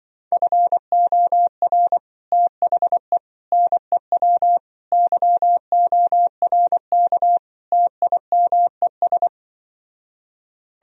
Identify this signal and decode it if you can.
Morse code, heard as a keyed tone, about 24 wpm, 700 Hz. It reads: FOR THE NEW YORK TIMES
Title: Speed-Racing - Sets of 5 Words 20wpm